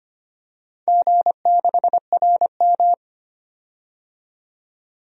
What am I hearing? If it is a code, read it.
Morse code, heard as a keyed tone, about 25 wpm, 700 Hz. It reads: G6RM
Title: DXCC-100 Call Signs - 4 Characters - 25wpm